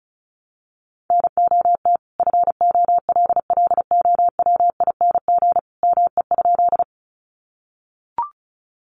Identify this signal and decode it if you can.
Morse code, heard as a keyed tone, about 34 wpm, 700 Hz. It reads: NOT FOLLOWING ME?
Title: Sentences from Top 700 Words 35wpm